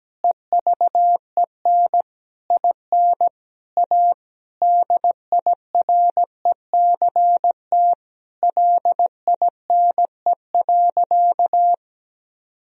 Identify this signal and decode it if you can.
Morse code, heard as a keyed tone, about 17 wpm, 700 Hz. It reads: EVEN IN A DIRECT LINE.
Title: Sentences from Top 1000 Words - Review Part 2 - 17wpm